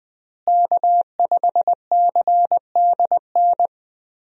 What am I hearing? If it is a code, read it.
Morse code, heard as a keyed tone, about 20 wpm, 700 Hz. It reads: K5CDN
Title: USA Call Signs - 5 Characters - Repeated 3x - 20wpm